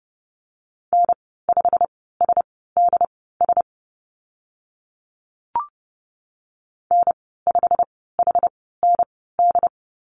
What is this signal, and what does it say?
Morse code, heard as a keyed tone, about 30 wpm, spaced as for 17 wpm, 700 Hz. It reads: N5SDS N5HND
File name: Lesson 15g - Number 5 - Call Signs 30wpm